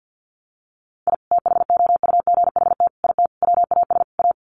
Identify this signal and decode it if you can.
Morse code, heard as a keyed tone, about 50 wpm, 700 Hz. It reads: I THOUGHT IT WAS A
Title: Sets of 5 Words 50wpm